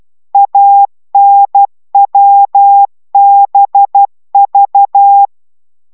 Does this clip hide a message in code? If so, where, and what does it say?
Yes: Morse code from 0.3 to 5.3 s, heard as a keyed tone, about 12 wpm, 800 Hz. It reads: ANWBV